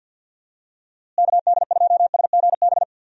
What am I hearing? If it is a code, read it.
Morse code, heard as a keyed tone, about 50 wpm, 700 Hz. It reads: KD2SGB
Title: USA Call Signs - 6 Characters - Repeated 2x - 50wpm